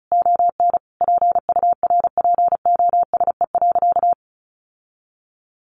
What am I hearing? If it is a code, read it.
Morse code, heard as a keyed tone, about 34 wpm, 700 Hz. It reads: ON PURPOSE.
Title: Sentences from Top 800 Words 35wpm